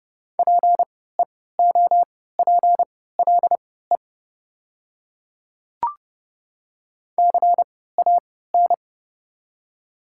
Morse code, heard as a keyed tone, about 29 wpm, spaced as for 17 wpm, 700 Hz. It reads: PEOPLE CAN
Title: 3x Character Spacing - Top 100 Words - 30wpm